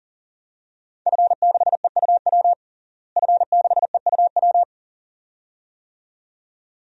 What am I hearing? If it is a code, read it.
Morse code, heard as a keyed tone, about 40 wpm, 700 Hz. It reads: F6EUW F6EUW
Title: DXCC-100 Call Signs - 5 Characters - Repeated 2x - 40wpm